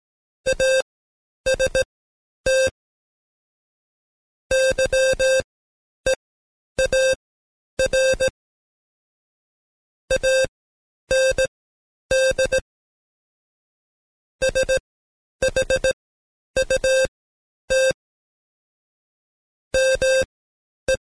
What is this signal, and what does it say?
Morse code, heard as a keyed tone, about 17 wpm, spaced as for 9 wpm, 530 Hz. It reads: AST YEAR AND SHUT ME
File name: Code Quick / AudioDownld2 / Seg131-Stepper4 9WPM